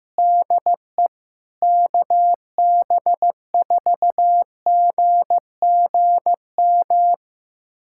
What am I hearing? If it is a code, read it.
Morse code, heard as a keyed tone, about 15 wpm, 700 Hz. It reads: DE KB4GGM
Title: USA Callsign DE Callsign - 6 Characters - 15wpm